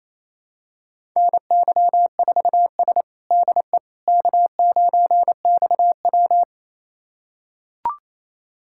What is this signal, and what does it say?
Morse code, heard as a keyed tone, about 27 wpm, 700 Hz. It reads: NY4S DE K9XW
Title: USA Callsign DE Callsign - 4 Characters - 28wpm